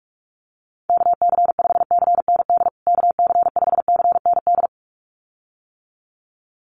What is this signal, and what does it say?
Morse code, heard as a keyed tone, about 44 wpm, 700 Hz. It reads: KC5CND KC5CND
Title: USA Call Signs Repeated 2x 45wpm